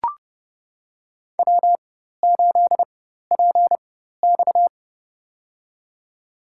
Morse code, heard as a keyed tone, about 30 wpm, spaced as for 14 wpm, 700 Hz. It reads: W8PX
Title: USA Call Signs - 4 Characters - 4x Character Spacing - 30wpm